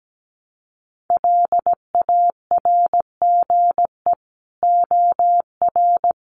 Morse code, heard as a keyed tone, about 17 wpm, 700 Hz. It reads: LARGE OR
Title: Sentences from Top 200 Words 17wpm